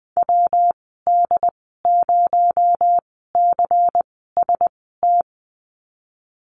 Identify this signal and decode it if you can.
Morse code, heard as a keyed tone, about 20 wpm, spaced as for 14 wpm, 700 Hz. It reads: WD0CST
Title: 245 Lesson 39g - Letter z - Call Signs 20wpm